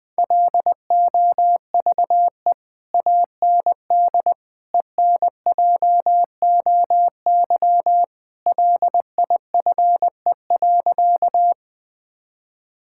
Morse code, heard as a keyed tone, about 20 wpm, 700 Hz. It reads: LOVE AND ENJOY LIFE.
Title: Sentences from Top 1000 Words - Review Part 5 - 20wpm